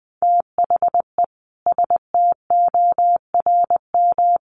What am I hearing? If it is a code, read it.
Morse code, heard as a keyed tone, about 20 wpm, 700 Hz. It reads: THE STORM
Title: Sentences from Top 1500 Words 20wpm